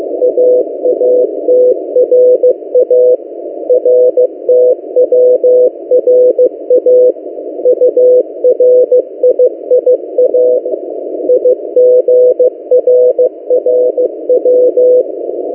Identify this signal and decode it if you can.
Morse code, heard as a keyed tone, about 15 wpm, 500 Hz. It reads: AATRA RTWRA URIIA IGRRW
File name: M08a- -1355utc-8096khz